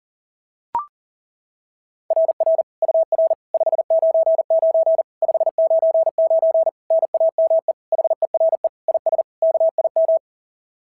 Morse code, heard as a keyed tone, about 38 wpm, 660 Hz. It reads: RR UR 599 599 NAME HERE IS KIM
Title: SOTA QSO Practice 40wpm